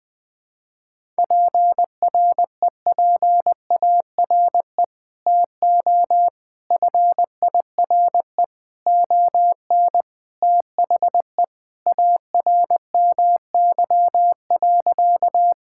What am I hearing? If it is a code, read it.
Morse code, heard as a keyed tone, about 20 wpm, 700 Hz. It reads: PREPARE TO FIRE ON THE ARMY.
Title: Sentences from Top 1000 Words - Review Part 6 - 20wpm